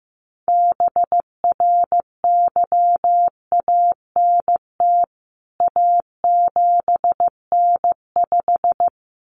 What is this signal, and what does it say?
Morse code, heard as a keyed tone, about 15 wpm, 700 Hz. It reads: BRYANT A7N5
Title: CWT Exchange 15wpm